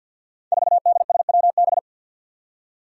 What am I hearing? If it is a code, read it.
Morse code, heard as a keyed tone, about 53 wpm, 700 Hz. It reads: 4DSWB
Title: Five Challenging Letter-Number Combinations 50wpm